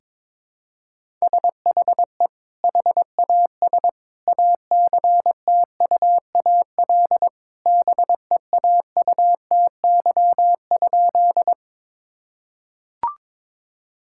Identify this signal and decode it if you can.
Morse code, heard as a keyed tone, about 22 wpm, 700 Hz. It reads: SHE HAS ACTUAL BEAUTY?
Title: Sentences from Top 1000 Words - Review Part 5 - 22wpm